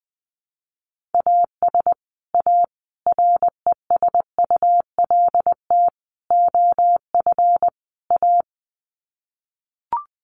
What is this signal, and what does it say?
Morse code, heard as a keyed tone, about 20 wpm, 700 Hz. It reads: AS A RESULT OF A
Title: Sets of 5 Words Encore 20wpm